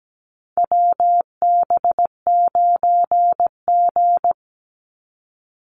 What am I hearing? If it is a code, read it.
Morse code, heard as a keyed tone, about 17 wpm, 700 Hz. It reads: WB9G
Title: Call Signs - 17wpm